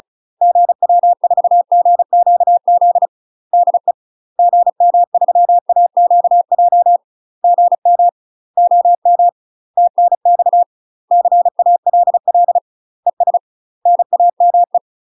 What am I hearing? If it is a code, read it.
Morse code, heard as a keyed tone, about 35 wpm, 700 Hz. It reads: GW4GQZ DE GM3AQJ GM OM TNX CALL ES NAME
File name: UK MASTER QSOs TRAINING / 35WPM / UK MASTER QSOs 0000